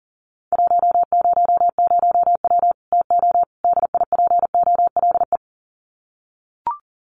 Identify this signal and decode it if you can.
Morse code, heard as a keyed tone, about 40 wpm, 700 Hz. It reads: 100W TO DIPOLE
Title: Common QSO Phrases 40wpm